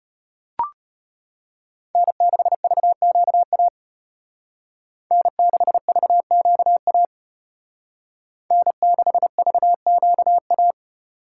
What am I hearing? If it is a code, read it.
Morse code, heard as a keyed tone, about 30 wpm, 700 Hz. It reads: N6VQA N6VQA N6VQA
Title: Speed Racing - USA 1x3 Call Signs 25wpm